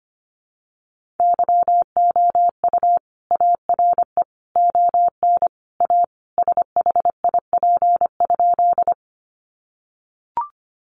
Morse code, heard as a keyed tone, about 25 wpm, 700 Hz. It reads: YOU ARE ON A SHIP?
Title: Sentences from Top 400 Words 25wpm